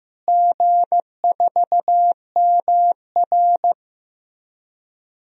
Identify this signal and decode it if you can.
Morse code, heard as a keyed tone, about 15 wpm, 700 Hz. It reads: G4MR
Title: DXCC-100 Call Signs - 4 Characters - Repeated 3x - 15wpm